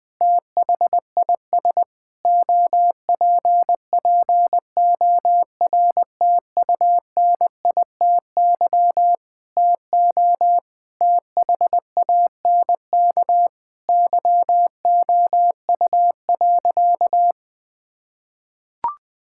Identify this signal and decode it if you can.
Morse code, heard as a keyed tone, about 20 wpm, 700 Hz. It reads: THIS OPPORTUNITY TO THANK YOU.
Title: Sentences with 8-Word Phrases - 20wpm